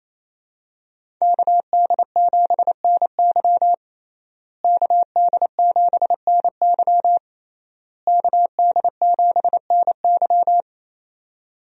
Morse code, heard as a keyed tone, about 28 wpm, 700 Hz. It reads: KD7NY KD7NY KD7NY
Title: USA Call Signs - 5 Characters - Repeated 3x - 28wpm